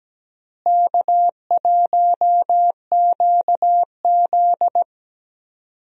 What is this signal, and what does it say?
Morse code, heard as a keyed tone, about 17 wpm, 700 Hz. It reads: K1QZ
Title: USA Call Signs - 4 Characters - Repeated 3x - 17wpm